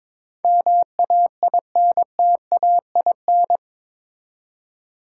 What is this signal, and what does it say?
Morse code, heard as a keyed tone, about 22 wpm, 700 Hz. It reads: MAINTAIN
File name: practice / 22wpm / Eight Letter Words - 22wpm